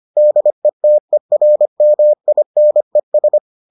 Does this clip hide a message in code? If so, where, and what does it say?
Yes: Morse code from 0.2 to 3.4 s, heard as a keyed tone, about 25 wpm, 600 Hz. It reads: DETERMINES